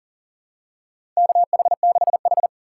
Morse code, heard as a keyed tone, about 40 wpm, 700 Hz. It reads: KH6H
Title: USA Call Signs - 4 Characters - 4x Character Spacing - 40wpm